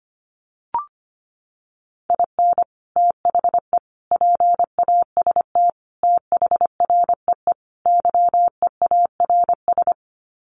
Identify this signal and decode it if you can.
Morse code, heard as a keyed tone, about 25 wpm, 700 Hz. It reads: IN THE PAST THREE YEARS
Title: Sets of 5 Words Encore2 25wpm